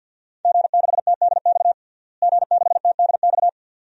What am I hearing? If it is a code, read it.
Morse code, heard as a keyed tone, about 51 wpm, 700 Hz. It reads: G6TDX G6TDX
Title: DXCC-100 Call Signs - 5 Characters - Repeated 2x - 50wpm